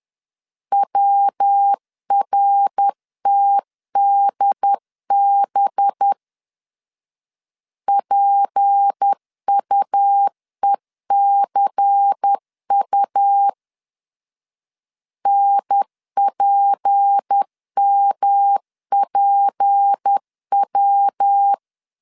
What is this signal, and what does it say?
Morse code, heard as a keyed tone, about 10 wpm, 800 Hz. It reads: WRTDB PUECU NPMPW
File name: LEZIONI / 6 PKX0 / 6 P 10wpm 2432